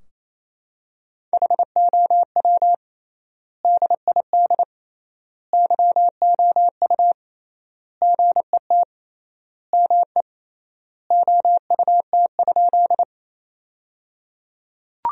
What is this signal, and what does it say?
Morse code, heard as a keyed tone, about 28 wpm, 700 Hz. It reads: HOW DID YOU GET ME OUT?
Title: 3x Word-Spacing - Sentences from Top 200 Words - 28wpm